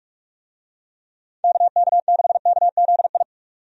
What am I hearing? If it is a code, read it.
Morse code, heard as a keyed tone, about 46 wpm, 700 Hz. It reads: KK6KZI
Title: USA Call Signs - 6 Characters - 45wpm